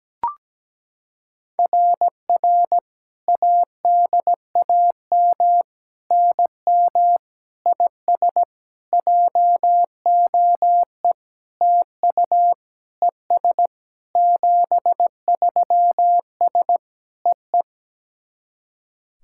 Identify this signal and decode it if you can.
Morse code, heard as a keyed tone, about 17 wpm, 700 Hz. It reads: RR ADAM NM IS JOE TU ES 73S EE
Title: SOTA QSO Practice 17wpm